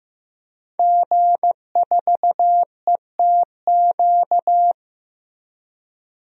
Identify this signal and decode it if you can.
Morse code, heard as a keyed tone, about 15 wpm, 700 Hz. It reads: G4ETQ
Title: DXCC-100 Call Signs - 5 Characters - 15wpm